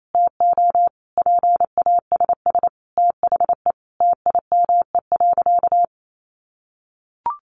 Morse code, heard as a keyed tone, about 28 wpm, 700 Hz. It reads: TO PASS THE TIME.
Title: Sentences from Top 700 Words 28wpm